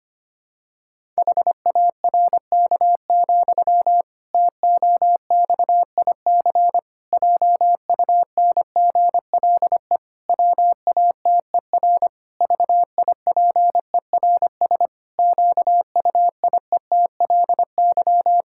Morse code, heard as a keyed tone, about 25 wpm, 700 Hz. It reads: HARK, TOXIC JUNGLE WATER VIPERS QUIETLY
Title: Pangram Challenge v2 - 25wpm